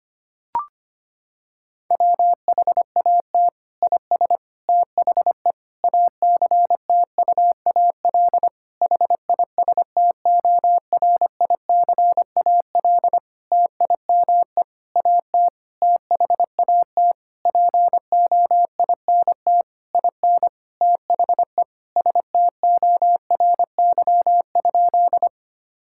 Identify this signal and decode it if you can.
Morse code, heard as a keyed tone, about 25 wpm, 700 Hz. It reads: WHAT IS THE ACTUAL HISTORICAL TIME AT THAT POINT IN THE STORY?
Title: Sentences from Top 1000 Words - Review Part 5 - 25wpm